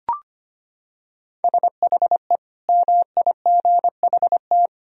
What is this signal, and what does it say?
Morse code, heard as a keyed tone, about 25 wpm, 700 Hz. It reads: SHE MIGHT
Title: Sentences from Top 1000 Words - Review Part 4 - 25wpm